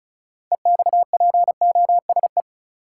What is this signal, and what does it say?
Morse code, heard as a keyed tone, about 36 wpm, 700 Hz. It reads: EXPOSE
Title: Six Letter Words - 35wpm